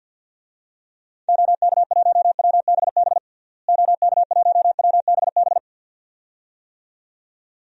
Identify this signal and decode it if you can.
Morse code, heard as a keyed tone, about 49 wpm, 700 Hz. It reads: OK1WBB OK1WBB
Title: DXCC-100 Call Signs - 6 Characters - Repeated 2x - 50wpm